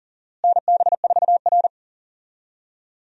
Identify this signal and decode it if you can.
Morse code, heard as a keyed tone, about 40 wpm, 700 Hz. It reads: NB4R
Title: USA Call Signs - 4 Characters - Repeated 2x - 40wpm